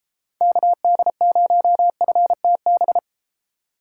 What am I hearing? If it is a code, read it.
Morse code, heard as a keyed tone, about 30 wpm, 700 Hz. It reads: KD0FTB